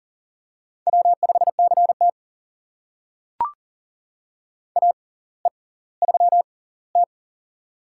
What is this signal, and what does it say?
Morse code, heard as a keyed tone, about 40 wpm, spaced as for 21 wpm, 700 Hz. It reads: W5CT AE3T
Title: USA Call Signs - 4 Characters - 6x Character Spacing - 40wpm